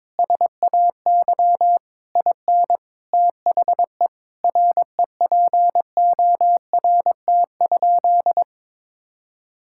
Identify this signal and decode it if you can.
Morse code, heard as a keyed tone, about 22 wpm, 700 Hz. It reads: SAY IN THE REPORT?
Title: Sentences from Top 1000 Words - Review Part 6 - 22wpm